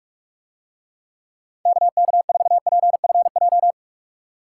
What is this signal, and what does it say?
Morse code, heard as a keyed tone, about 47 wpm, 700 Hz. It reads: KK4PFJ